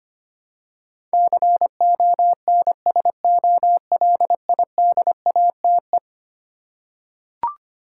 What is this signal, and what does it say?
Morse code, heard as a keyed tone, about 25 wpm, 700 Hz. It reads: CONSOLIDATE
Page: morse-code-ninja-practice-25wpm